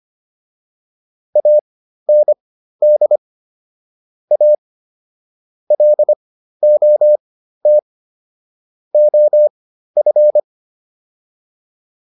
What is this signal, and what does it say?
Morse code, heard as a keyed tone, about 25 wpm, spaced as for 13 wpm, 600 Hz. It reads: AND A LOT OF